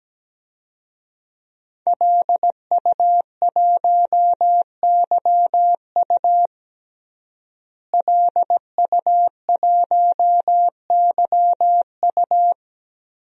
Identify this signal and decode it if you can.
Morse code, heard as a keyed tone, about 17 wpm, 700 Hz. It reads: LU1YU LU1YU
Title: DXCC-100 Call Signs - 5 Characters - Repeated 2x - 17wpm